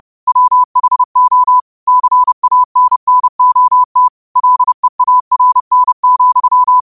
Morse code, heard as a keyed tone, about 30 wpm, 1000 Hz. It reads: WHO CANNOT LEARN,
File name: qotdfiles / 30wpm